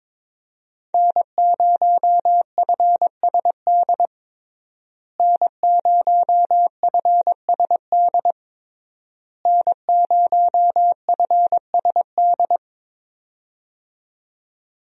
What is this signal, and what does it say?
Morse code, heard as a keyed tone, about 22 wpm, 700 Hz. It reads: N0FSD N0FSD N0FSD
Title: USA Call Signs - 5 Characters - Repeated 3x - 22wpm